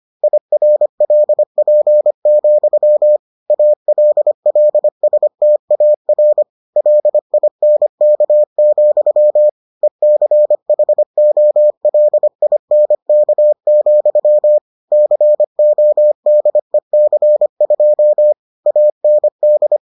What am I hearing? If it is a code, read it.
Morse code, heard as a keyed tone, about 25 wpm, 600 Hz. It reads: IRLP, ALLSTAR LINK, ECHOLINK, CODEC2 AND